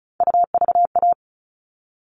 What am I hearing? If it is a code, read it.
Morse code, heard as a keyed tone, about 34 wpm, 700 Hz. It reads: UVA
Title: All Cut Numbers - 3-Digit 35wpm